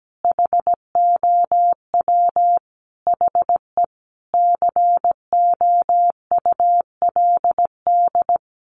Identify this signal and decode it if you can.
Morse code, heard as a keyed tone, about 17 wpm, 700 Hz. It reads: HOW HE COULD
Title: Sentences from Top 1400 Words 17wpm